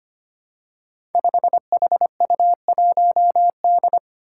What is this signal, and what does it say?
Morse code, heard as a keyed tone, about 25 wpm, 700 Hz. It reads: 5HU1D
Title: Five Challenging Letter-Number Combinations 25wpm